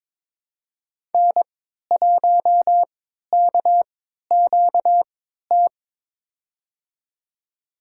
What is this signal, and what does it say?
Morse code, heard as a keyed tone, about 22 wpm, spaced as for 12 wpm, 700 Hz. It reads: N1KQT
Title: USA Call Signs - 5 Characters - 3x Character Spacing - 22wpm